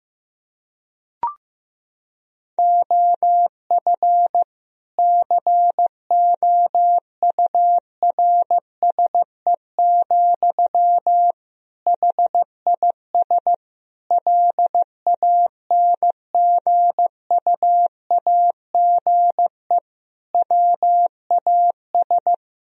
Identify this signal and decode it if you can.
Morse code, heard as a keyed tone, about 15 wpm, 700 Hz. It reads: OF COURSE, HIS LANGUAGE WAS
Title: Sentences from Top 1000 Words - Review Part 3 - 15wpm